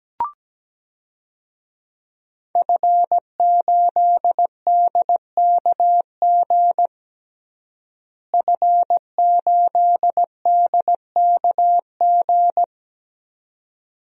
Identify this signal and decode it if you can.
Morse code, heard as a keyed tone, about 17 wpm, 700 Hz. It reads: F8DKG F8DKG
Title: DXCC-100 Call Signs - 5 Characters - Repeated 2x - 17wpm